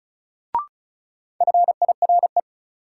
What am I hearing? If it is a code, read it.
Morse code, heard as a keyed tone, about 34 wpm, 700 Hz. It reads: FIRE
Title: Sentences from Top 400 Words - New Word Drill - 35wpm